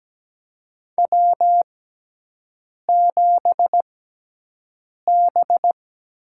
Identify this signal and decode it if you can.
Morse code, heard as a keyed tone, about 17 wpm, spaced as for 6 wpm, 700 Hz. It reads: W7B
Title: USA Call Signs - 4 Characters - 6x Character Spacing - 17wpm